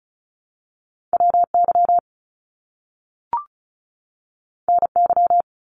Morse code, heard as a keyed tone, about 36 wpm, 700 Hz. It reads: WY NY
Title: US States 35wpm